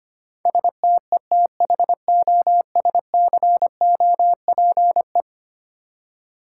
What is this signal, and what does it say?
Morse code, heard as a keyed tone, about 25 wpm, 700 Hz. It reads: STETHOSCOPE
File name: practice / 25wpm / Easy Words with Hints - 25wpm